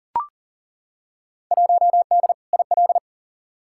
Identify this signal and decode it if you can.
Morse code, heard as a keyed tone, about 40 wpm, 700 Hz. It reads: 1D IL
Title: Field Day Exchanges - 40wpm